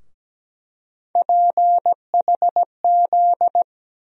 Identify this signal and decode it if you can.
Morse code, heard as a keyed tone, about 17 wpm, 700 Hz. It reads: PHZ
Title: USA Call Sign Postfixes - 3 Letters 17wpm